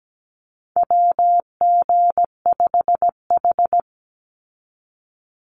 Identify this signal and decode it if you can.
Morse code, heard as a keyed tone, about 17 wpm, 700 Hz. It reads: WG5H
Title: Call Signs - 17wpm